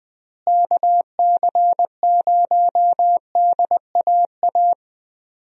Morse code, heard as a keyed tone, about 20 wpm, 700 Hz. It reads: KC0DAA
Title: USA Call Signs - 6 Characters - Repeated 3x - 20wpm